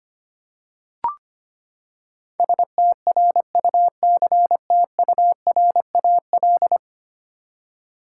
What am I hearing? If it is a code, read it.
Morse code, heard as a keyed tone, about 25 wpm, 700 Hz. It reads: STRUCTURAL
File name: Ten Letter Words - 25wpm